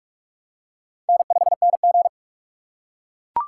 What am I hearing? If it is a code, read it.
Morse code, heard as a keyed tone, about 44 wpm, 700 Hz. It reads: N5NG
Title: USA Call Signs - 4 Characters - 2x Character Spacing - 45wpm